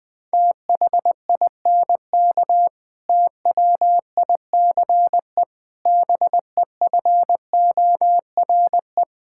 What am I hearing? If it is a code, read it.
Morse code, heard as a keyed tone, about 20 wpm, 700 Hz. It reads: THINK TWICE BEFORE
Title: Difficult Sentences with Hints - 20wpm